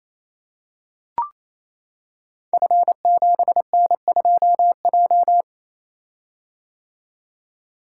Morse code, heard as a keyed tone, about 27 wpm, 700 Hz. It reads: F7N2J
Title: Code Groups - 5 Letter:Numbers - 28wpm